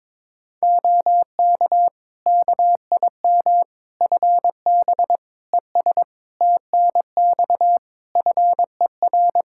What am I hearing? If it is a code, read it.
Morse code, heard as a keyed tone, about 22 wpm, 700 Hz. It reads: OK KIM FB ES TNX FER
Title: Practice QSO - Exchange 2 - Info - 22wpm